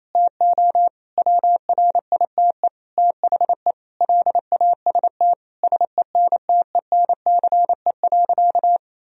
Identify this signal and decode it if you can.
Morse code, heard as a keyed tone, about 28 wpm, 700 Hz. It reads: TO WRITE THE LAST SENTENCE.
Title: Sentences from Top 1000 Words - Review Part 1 - 28wpm